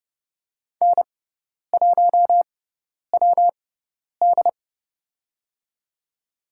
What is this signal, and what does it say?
Morse code, heard as a keyed tone, about 30 wpm, spaced as for 10 wpm, 700 Hz. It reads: N1WD
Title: USA Call Signs - 4 Characters - 6x Character Spacing - 30wpm